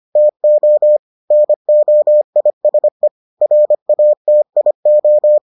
Morse code, heard as a keyed tone, about 25 wpm, 600 Hz. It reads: TO NOISE RATIO